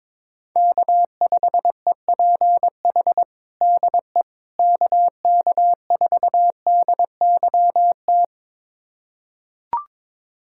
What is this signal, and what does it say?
Morse code, heard as a keyed tone, about 21 wpm, 700 Hz. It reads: K5EPH DE KK4DYT
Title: USA Callsign DE Callsign 22wpm